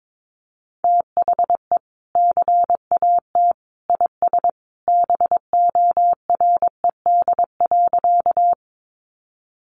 Morse code, heard as a keyed tone, about 22 wpm, 700 Hz. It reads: THE CAT IS BORED.
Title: Easy Phrases 22wpm